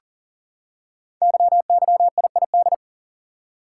Morse code, heard as a keyed tone, about 39 wpm, 700 Hz. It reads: YYIID
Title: Code Groups - 5 Letter:Numbers - 40wpm